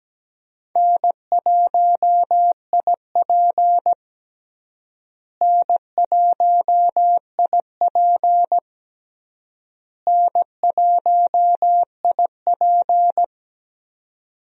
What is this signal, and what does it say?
Morse code, heard as a keyed tone, about 17 wpm, 700 Hz. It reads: N1IP N1IP N1IP
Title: USA Call Signs - 4 Characters - Repeated 3x - 17wpm